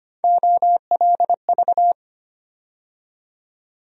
Morse code, heard as a keyed tone, about 25 wpm, 700 Hz. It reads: OLV
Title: USA Call Sign Postfixes - 3 Letters 25wpm